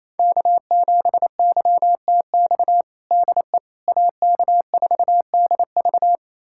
Morse code, heard as a keyed tone, about 28 wpm, 700 Hz. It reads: K7YTX DE AK4DV
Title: USA Callsign DE Callsign - 5 Characters - 28wpm